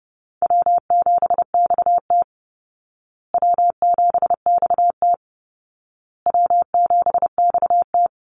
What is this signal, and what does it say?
Morse code, heard as a keyed tone, about 28 wpm, 700 Hz. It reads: W7XT W7XT W7XT
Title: Easy Call Signs Repeated 3x - 30wpm